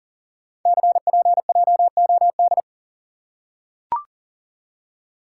Morse code, heard as a keyed tone, about 40 wpm, 700 Hz. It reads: CPJOD
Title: Code Groups - 5 Letters - 40wpm